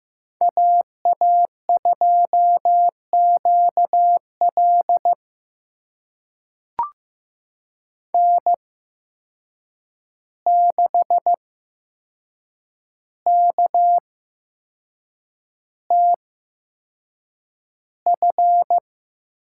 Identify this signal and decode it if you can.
Morse code, heard as a keyed tone, about 15 wpm, spaced as for 6 wpm, 700 Hz. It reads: AA2QL N6KTF
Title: USA Call Signs - 5 Characters - 8x Character Spacing - 15wpm